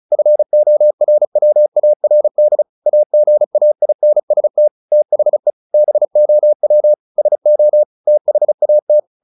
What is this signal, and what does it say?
Morse code, heard as a keyed tone, about 35 wpm, 600 Hz. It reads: FORWARD AGAINST THE BOW SO THAT
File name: the old man and the sea / the old man and the sea 1 35wpm